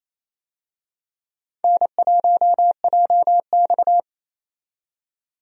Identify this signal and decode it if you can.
Morse code, heard as a keyed tone, about 28 wpm, 700 Hz. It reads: N1JX
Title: USA Call Signs - 4 Characters - 28wpm